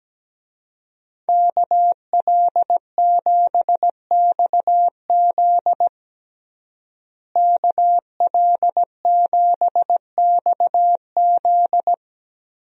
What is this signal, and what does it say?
Morse code, heard as a keyed tone, about 17 wpm, 700 Hz. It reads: KL7XZ KL7XZ
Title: DXCC-100 Call Signs - 5 Characters - Repeated 3x - 17wpm